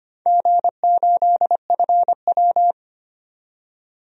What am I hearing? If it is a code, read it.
Morse code, heard as a keyed tone, about 25 wpm, 700 Hz. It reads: G8FW
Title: DXCC-100 Call Signs - 4 Characters - 25wpm